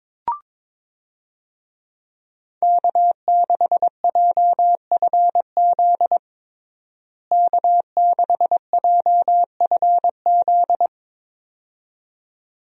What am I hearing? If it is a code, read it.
Morse code, heard as a keyed tone, about 22 wpm, 700 Hz. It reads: K6JFZ K6JFZ
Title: USA Call Signs - 5 Characters - Repeated 2x - 22wpm